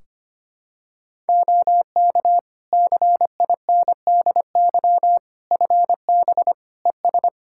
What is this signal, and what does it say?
Morse code, heard as a keyed tone, about 25 wpm, 700 Hz. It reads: OK CINDY FB ES
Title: K1USN SST QSO Practice 25wpm